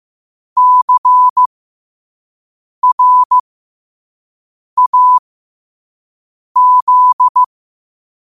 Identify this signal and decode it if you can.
Morse code, heard as a keyed tone, about 15 wpm, spaced as for 5 wpm, 1000 Hz. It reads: CRAZ
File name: qotdfiles / 5wpm